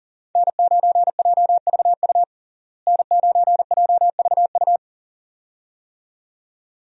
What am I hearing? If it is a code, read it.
Morse code, heard as a keyed tone, about 40 wpm, 700 Hz. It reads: N9JVU N9JVU
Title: USA Call Signs - 5 Characters - Repeated 2x - 40wpm